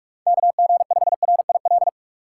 Morse code, heard as a keyed tone, about 46 wpm, 700 Hz. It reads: KG5RIL
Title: USA Call Signs - 6 Characters - 2x Character Spacing - 45wpm